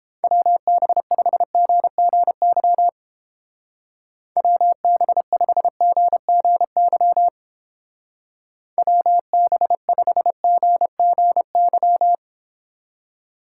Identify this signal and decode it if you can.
Morse code, heard as a keyed tone, about 30 wpm, 700 Hz. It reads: WB5GGY WB5GGY WB5GGY